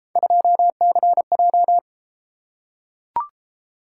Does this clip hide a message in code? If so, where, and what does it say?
Yes: Morse code from 0.2 to 1.8 s, heard as a keyed tone, about 30 wpm, 700 Hz. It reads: 2CJ